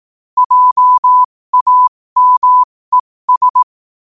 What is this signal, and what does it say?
Morse code, heard as a keyed tone, about 18 wpm, spaced as for 15 wpm, 1000 Hz. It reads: JAMES
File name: qotdfiles / 15wpm